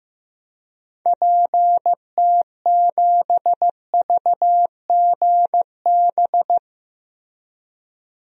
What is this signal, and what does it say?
Morse code, heard as a keyed tone, about 15 wpm, 700 Hz. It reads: PT7VGB
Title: DXCC-100 Call Signs - 6 Characters - Repeated 2x - 15wpm